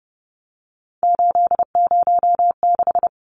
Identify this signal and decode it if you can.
Morse code, heard as a keyed tone, about 30 wpm, 700 Hz. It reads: 806
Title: Numbers - 3-Digit 30wpm